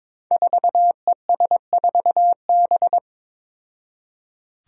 Morse code, heard as a keyed tone, about 21 wpm, 700 Hz. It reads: 4ES4B
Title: Five Challenging Letter-Number Combinations 22wpm